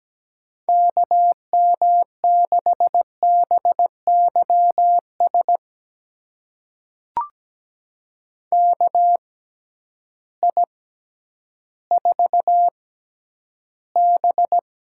Morse code, heard as a keyed tone, about 17 wpm, spaced as for 10 wpm, 700 Hz. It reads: KM6BYS KI4B
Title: USA Call Signs - 6 Characters - 6x Character Spacing - 17wpm